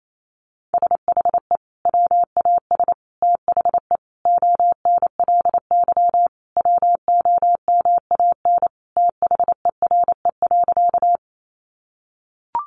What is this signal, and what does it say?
Morse code, heard as a keyed tone, about 28 wpm, 700 Hz. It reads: SHE WAS THE ONLY WOMAN THERE.
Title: Sentences from Top 1100 Words 28wpm